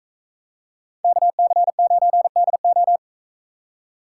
Morse code, heard as a keyed tone, about 42 wpm, 700 Hz. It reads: KC9DO